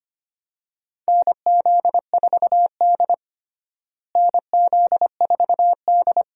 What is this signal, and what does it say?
Morse code, heard as a keyed tone, about 25 wpm, 700 Hz. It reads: NZ4D NZ4D
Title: USA Call Signs - 4 Characters - Repeated 2x - 25wpm